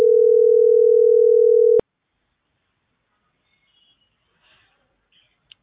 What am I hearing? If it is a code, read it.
Ringback tone (North America), 440 + 480 Hz.